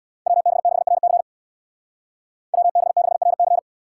Morse code, heard as a keyed tone, about 60 wpm, 700 Hz. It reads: WB6RC WB6RC
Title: Speed Racing - USA Call Signs 50wpm